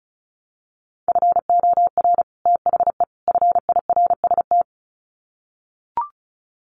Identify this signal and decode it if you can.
Morse code, heard as a keyed tone, about 34 wpm, 700 Hz. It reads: FOR THE FIRST
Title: Sets of 3 Words 35wpm